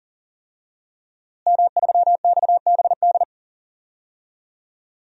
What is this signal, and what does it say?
Morse code, heard as a keyed tone, about 40 wpm, 700 Hz. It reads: M3XBD
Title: DXCC-100 Call Signs - 5 Characters - Repeated 3x - 40wpm